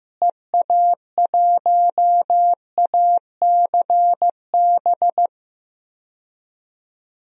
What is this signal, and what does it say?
Morse code, heard as a keyed tone, about 15 wpm, 700 Hz. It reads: EA1ACB
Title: DXCC-100 Call Signs - 6 Characters - Repeated 3x - 15wpm